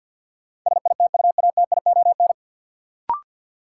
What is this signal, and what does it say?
Morse code, heard as a keyed tone, about 53 wpm, 700 Hz. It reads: SITUATION
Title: Sentences from Top 1200 Words - New Word Drill Alt - 50wpm